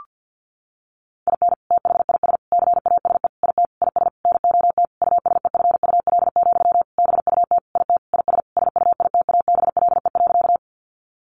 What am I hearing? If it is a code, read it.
Morse code, heard as a keyed tone, about 50 wpm, 700 Hz. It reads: IN THIS CASE IT IS NOT USEFUL, BUT IT IS SUITABLE.
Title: Sentences from Top 1400 Words 50wpm